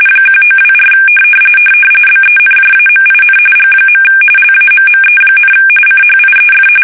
RTTY, about 45.45 baud. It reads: 1924 BENJAMIN G. LAMME, CHIEF
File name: fsk850 / benjamin-g-lamme